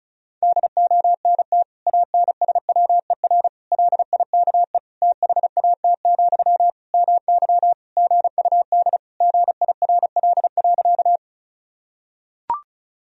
Morse code, heard as a keyed tone, about 36 wpm, 700 Hz. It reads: DONT ANSWER LIKE THAT, MY GUD GIRL.
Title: Sentences from Top 400 Words and Abbreviations 35wpm